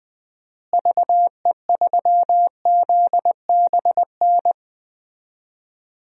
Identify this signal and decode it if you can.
Morse code, heard as a keyed tone, about 20 wpm, 700 Hz. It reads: VE3ZBN
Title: DXCC-100 Call Signs - 6 Characters - Repeated 2x - 20wpm